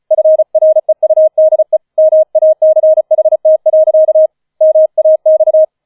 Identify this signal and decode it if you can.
Morse code, heard as a keyed tone, about 35 wpm, 630 Hz. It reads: FREUDE MACHT. MAX
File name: 005 Fünfter Streich 35wpm